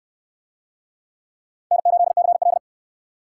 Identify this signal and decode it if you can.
Morse code, heard as a keyed tone, about 69 wpm, 700 Hz. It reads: N8CL